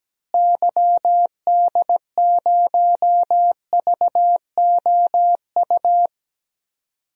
Morse code, heard as a keyed tone, about 17 wpm, 700 Hz. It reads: YD0VOU
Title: DXCC-100 Call Signs - 6 Characters - Repeated 3x - 17wpm